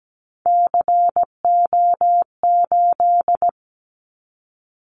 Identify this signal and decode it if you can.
Morse code, heard as a keyed tone, about 17 wpm, 700 Hz. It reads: CO8
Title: Callsign Prefixes for Top 100 DXCC - 17wpm